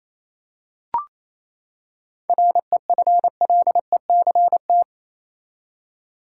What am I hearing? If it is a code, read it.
Morse code, heard as a keyed tone, about 28 wpm, 700 Hz. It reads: REFLECT
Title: Seven Letter Words - 28wpm